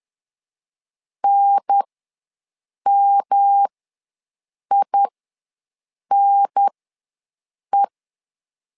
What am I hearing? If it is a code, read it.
Morse code, heard as a keyed tone, about 10 wpm, spaced as for 6 wpm, 800 Hz. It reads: NMINE
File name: LEZIONI / 3 MWRO / 3 M ws 10wpm 2121